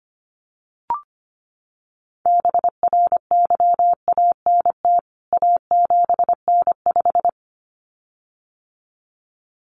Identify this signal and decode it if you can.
Morse code, heard as a keyed tone, about 25 wpm, 700 Hz. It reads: BRYANT A7N5
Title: CWT Exchange 25wpm